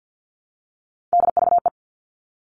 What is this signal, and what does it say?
Morse code, heard as a keyed tone, about 50 wpm, 700 Hz. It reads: DVE
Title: All Cut Numbers - 3-Digit 50wpm